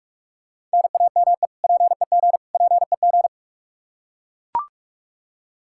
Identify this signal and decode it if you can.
Morse code, heard as a keyed tone, about 41 wpm, 700 Hz. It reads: NAME PEG PEG
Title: Practice QSO - Exchange 1 - Beginning - 45wpm